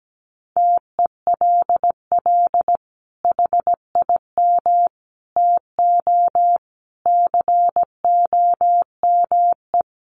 Morse code, heard as a keyed tone, about 17 wpm, 700 Hz. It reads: TELL HIM TO COME
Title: Sentences from Top 700 Words 17wpm